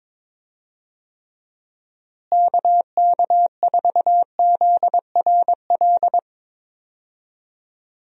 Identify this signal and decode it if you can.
Morse code, heard as a keyed tone, about 22 wpm, 700 Hz. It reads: KK4ZRL
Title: USA Call Signs - 6 Characters - Repeated 2x - 22wpm